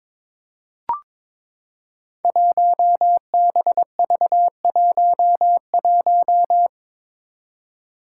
Morse code, heard as a keyed tone, about 22 wpm, 700 Hz. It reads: 1BV11
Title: Code Groups - 5 Letter:Numbers - 22wpm